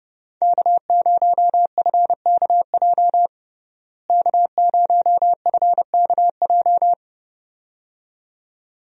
Morse code, heard as a keyed tone, about 29 wpm, 700 Hz. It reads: K0FKJ K0FKJ
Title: USA Call Signs - 5 Characters - Repeated 2x - 30wpm